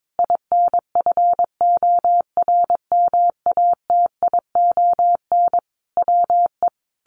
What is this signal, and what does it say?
Morse code, heard as a keyed tone, about 22 wpm, 700 Hz. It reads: INFORMATION WE
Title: Sentences from Top 1300 Words 22wpm